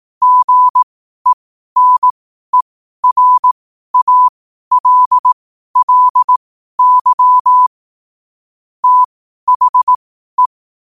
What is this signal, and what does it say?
Morse code, heard as a keyed tone, about 18 wpm, spaced as for 12 wpm, 1000 Hz. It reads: GENERALLY THE
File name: qotdfiles / 12wpm